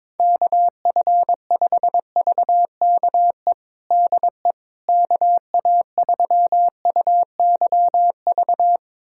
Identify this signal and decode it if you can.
Morse code, heard as a keyed tone, about 22 wpm, 700 Hz. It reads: KF5VKE DE KA3UYV
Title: USA Callsign DE Callsign - 6 Characters - 22wpm